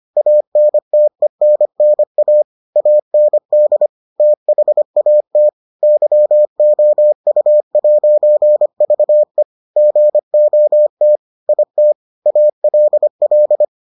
Morse code, heard as a keyed tone, about 25 wpm, 600 Hz. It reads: ANTENNA AND THAT YOU'VE GOT IT ALL